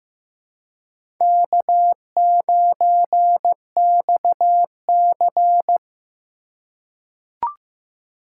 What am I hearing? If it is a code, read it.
Morse code, heard as a keyed tone, about 16 wpm, 700 Hz. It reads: K9XC
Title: USA Call Signs - 4 Characters - 3x Character Spacing - 15wpm